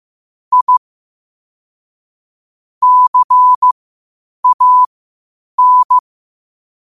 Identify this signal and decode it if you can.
Morse code, heard as a keyed tone, about 15 wpm, spaced as for 8 wpm, 1000 Hz. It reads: I CAN
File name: qotdfiles / 8wpm